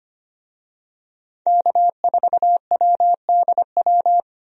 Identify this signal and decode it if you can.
Morse code, heard as a keyed tone, about 25 wpm, 700 Hz. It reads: K4WDW
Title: USA Call Signs - 5 Characters - 25wpm